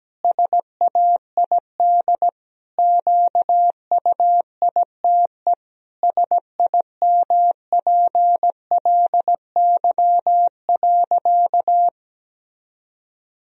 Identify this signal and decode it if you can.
Morse code, heard as a keyed tone, about 17 wpm, 700 Hz. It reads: SAID QUITE SIMPLY.
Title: Sentences from Top 1000 Words - Review Part 4 - 17wpm